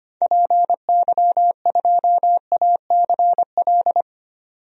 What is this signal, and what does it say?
Morse code, heard as a keyed tone, about 25 wpm, 700 Hz. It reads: PY2ACL
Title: DXCC-100 Call Signs - 6 Characters - 25wpm